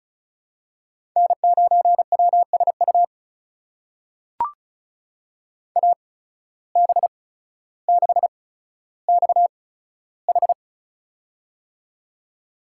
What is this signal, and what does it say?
Morse code, heard as a keyed tone, about 36 wpm, spaced as for 15 wpm, 700 Hz. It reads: N9WSU AB6XH
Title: USA Call Signs - 5 Characters - 8x Character Spacing - 35wpm